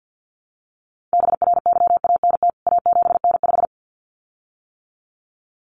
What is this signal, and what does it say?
Morse code, heard as a keyed tone, about 50 wpm, 700 Hz. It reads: BRYANT A7N5
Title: CWT Exchange 50wpm